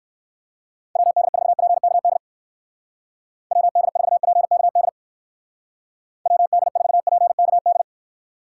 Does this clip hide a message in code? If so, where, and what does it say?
Yes: Morse code from 1.0 to 7.8 s, heard as a keyed tone, about 57 wpm, 700 Hz. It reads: WD4PKD WD4PKD WD4PKD